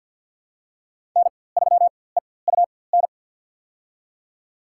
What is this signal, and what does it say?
Morse code, heard as a keyed tone, about 47 wpm, spaced as for 23 wpm, 700 Hz. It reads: N3EUN
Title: USA Call Signs - 5 Characters - 4x Character Spacing - 50wpm